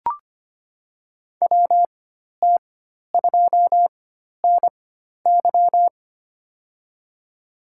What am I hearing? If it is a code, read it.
Morse code, heard as a keyed tone, about 25 wpm, spaced as for 12 wpm, 700 Hz. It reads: WT2NY
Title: USA Call Signs - 5 Characters - 4x Character Spacing - 25wpm